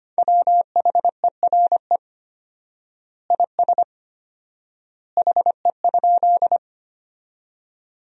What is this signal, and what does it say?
Morse code, heard as a keyed tone, about 25 wpm, 700 Hz. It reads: WHERE IS HE?
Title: 4x Word-Spacing - Sentences from Top 200 Words - 25wpm